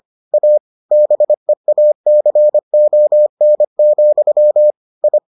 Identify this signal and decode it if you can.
Morse code, heard as a keyed tone, about 25 wpm, 600 Hz. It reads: A BEACON, I